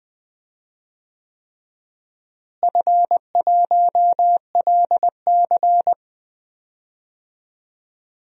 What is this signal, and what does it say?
Morse code, heard as a keyed tone, about 20 wpm, 700 Hz. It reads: F1LC
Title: DXCC-100 Call Signs - 4 Characters - Repeated 3x - 20wpm